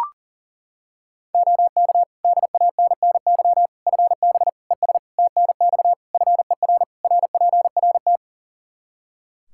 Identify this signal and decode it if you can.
Morse code, heard as a keyed tone, about 37 wpm, 700 Hz. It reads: OK DANNY FB ES TNX FER RPRT
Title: Practice QSO - Exchange 2 - Info - 40wpm